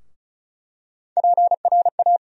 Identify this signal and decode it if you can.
Morse code, heard as a keyed tone, about 36 wpm, 700 Hz. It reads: PRA
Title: USA Call Sign Postfixes - 3 Letters 35wpm